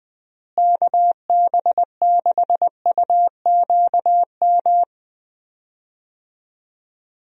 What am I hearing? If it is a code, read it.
Morse code, heard as a keyed tone, about 20 wpm, 700 Hz. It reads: KB6UQM
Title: USA Call Signs - 6 Characters - Repeated 2x - 20wpm